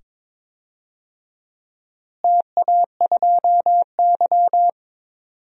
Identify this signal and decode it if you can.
Morse code, heard as a keyed tone, about 22 wpm, 700 Hz. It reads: TA2Y
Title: DXCC-100 Call Signs - 4 Characters - Repeated 2x - 22wpm